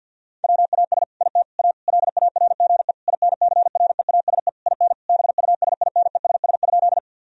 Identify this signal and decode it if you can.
Morse code, heard as a keyed tone, about 50 wpm, 700 Hz. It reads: WAS IT A LARGE INCREASE IN BUSINESS?
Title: Sentences from Top 1000 Words - Review Part 4 - 50wpm